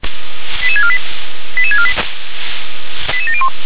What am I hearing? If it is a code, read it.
Five-tone selective call (ZVEI), heard as quick runs of five tones, tones of about 70 ms. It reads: 88648 88648 88701